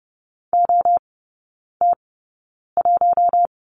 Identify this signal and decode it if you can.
Morse code, heard as a keyed tone, about 30 wpm, spaced as for 9 wpm, 700 Hz. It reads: OT1
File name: 081 Lesson 14e - Number 2 - Three Characters 30wpm